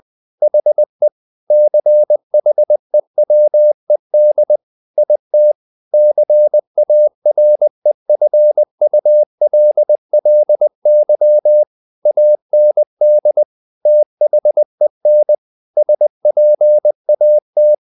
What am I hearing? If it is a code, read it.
Morse code, heard as a keyed tone, about 20 wpm, 600 Hz. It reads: HE CHEWED IT CAREFULLY AND THEN SPAT
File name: the old man and the sea / the old man and the sea part 1 20wpm